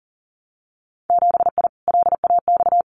Morse code, heard as a keyed tone, about 42 wpm, 700 Hz. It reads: 7I LAX
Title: Winter Field Day Exchanges Encore 40wpm